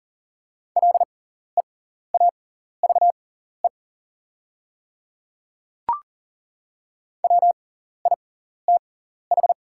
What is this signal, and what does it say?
Morse code, heard as a keyed tone, about 40 wpm, spaced as for 14 wpm, 690 Hz. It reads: LEAVE WITH
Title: 6x Character Spacing - Top 500 Words - 40wpm